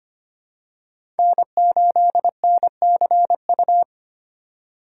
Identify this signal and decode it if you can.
Morse code, heard as a keyed tone, about 25 wpm, 700 Hz. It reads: N8NCU
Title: USA Call Signs - 5 Characters - 2x Character Spacing - 25wpm